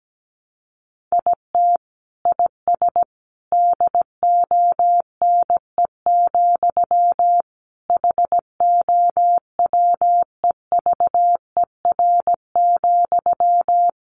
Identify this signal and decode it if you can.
Morse code, heard as a keyed tone, about 17 wpm, 700 Hz. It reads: IT IS DONE, HOWEVER,
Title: Sentences from Top 600 Words 17wpm